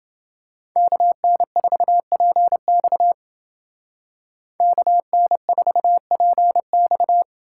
Morse code, heard as a keyed tone, about 29 wpm, 700 Hz. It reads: KN4PX KN4PX